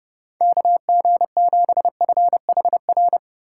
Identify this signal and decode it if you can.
Morse code, heard as a keyed tone, about 30 wpm, 700 Hz. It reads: KG7FHR
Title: USA Call Signs - 6 Characters - 4x Character Spacing - 30wpm